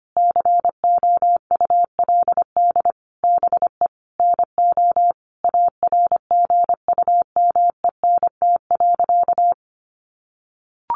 Morse code, heard as a keyed tone, about 25 wpm, 700 Hz. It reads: COULD BE NO ARGUMENT.
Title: A Sound Of Thunder 25wpm